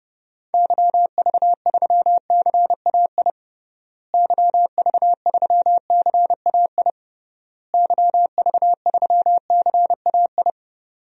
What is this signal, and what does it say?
Morse code, heard as a keyed tone, about 32 wpm, 700 Hz. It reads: YV3CAI YV3CAI YV3CAI
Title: DXCC-100 Call Signs - 6 Characters - Repeated 3x - 30wpm